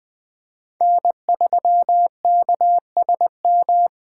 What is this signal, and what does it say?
Morse code, heard as a keyed tone, about 20 wpm, 700 Hz. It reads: N3KSM